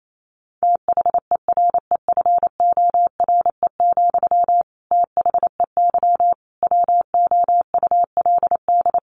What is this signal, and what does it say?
Morse code, heard as a keyed tone, about 28 wpm, 700 Hz. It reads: THEREFORE, THEY WOULD
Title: Sentences from Top 800 Words 28wpm